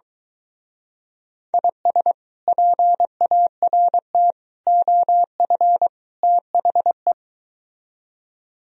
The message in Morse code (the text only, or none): IS PART OF THE